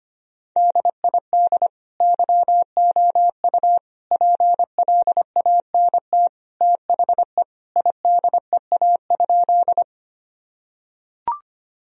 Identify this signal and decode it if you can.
Morse code, heard as a keyed tone, about 25 wpm, 700 Hz. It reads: DID YOU PLANT THE IDEA?
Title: Sentences from Top 1000 Words - Review Part 1 - 25wpm